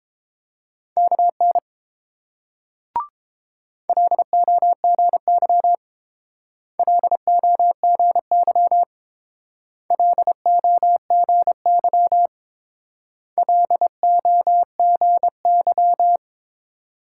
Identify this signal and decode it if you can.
Morse code, heard as a keyed tone, about 28 wpm, 700 Hz. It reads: KN LOGY LOGY LOGY LOGY